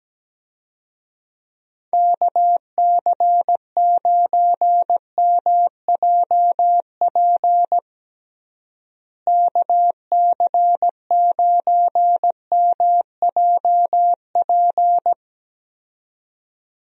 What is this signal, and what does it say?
Morse code, heard as a keyed tone, about 17 wpm, 700 Hz. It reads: KC9MJP KC9MJP
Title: USA Call Signs - 6 Characters - Repeated 2x - 17wpm